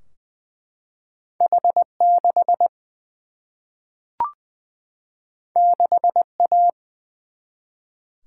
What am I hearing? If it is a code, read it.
Morse code, heard as a keyed tone, about 20 wpm, 700 Hz. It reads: H 6 6 A
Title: Single-Letter Single-Number 20wpm